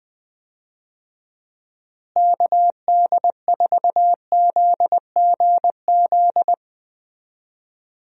Morse code, heard as a keyed tone, about 20 wpm, 700 Hz. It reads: KD4ZGZ
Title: USA Call Signs - 6 Characters - Repeated 3x - 20wpm